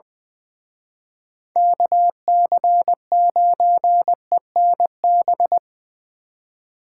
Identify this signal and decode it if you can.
Morse code, heard as a keyed tone, about 23 wpm, 700 Hz. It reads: KC9ENB
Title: Speed Racing - USA 2x3 Call Signs 20wpm